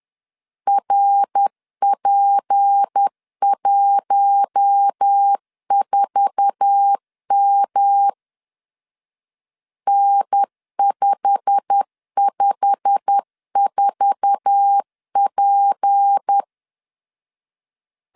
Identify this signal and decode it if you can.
Morse code, heard as a keyed tone, about 10 wpm, 800 Hz. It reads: RP14M N554P
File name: LEZIONI / 8 YJ45 / 8 YJ45 10wpm 2549